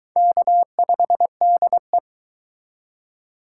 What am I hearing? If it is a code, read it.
Morse code, heard as a keyed tone, about 21 wpm, 700 Hz. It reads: K5DE